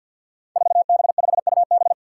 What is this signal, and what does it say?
Morse code, heard as a keyed tone, about 52 wpm, 700 Hz. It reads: 4B5UB
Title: Five Challenging Letter-Number Combinations 50wpm